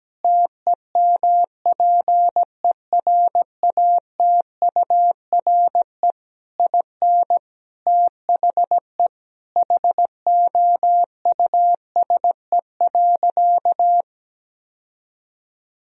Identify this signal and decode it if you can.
Morse code, heard as a keyed tone, about 17 wpm, 700 Hz. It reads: TEMPERATURE IN THE HOUSE.
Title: Sentences from Top 1000 Words - Review Part 4 - 17wpm